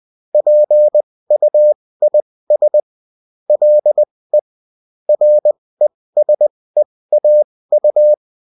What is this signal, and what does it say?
Morse code, heard as a keyed tone, about 20 wpm, spaced as for 16 wpm, 600 Hz. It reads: PUIS LE RESEAU